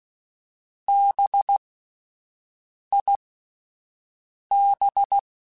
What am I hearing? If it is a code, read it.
Morse code, heard as a keyed tone, about 16 wpm, spaced as for 5 wpm, 800 Hz. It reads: BIB
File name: Spaced Alphas / Spaced Alphas mp3s / Bs sp @10